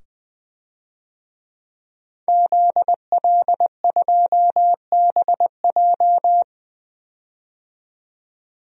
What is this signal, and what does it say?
Morse code, heard as a keyed tone, about 20 wpm, 700 Hz. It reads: ZL2BJ
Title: DXCC-100 Call Signs - 5 Characters - 20wpm